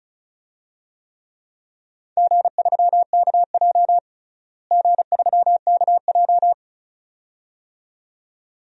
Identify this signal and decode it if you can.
Morse code, heard as a keyed tone, about 34 wpm, 700 Hz. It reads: G3KJ G3KJ
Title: DXCC-100 Call Signs - 4 Characters - Repeated 2x - 35wpm